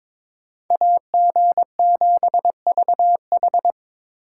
Morse code, heard as a keyed tone, about 26 wpm, 700 Hz. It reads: AG7VH
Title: Speed Racing - USA 2x2 Call Signs 22wpm